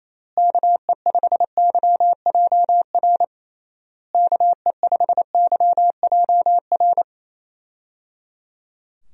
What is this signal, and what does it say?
Morse code, heard as a keyed tone, about 28 wpm, 700 Hz. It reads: KE5YJR KE5YJR
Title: USA Call Signs - 6 Characters - Repeated 3x - 28wpm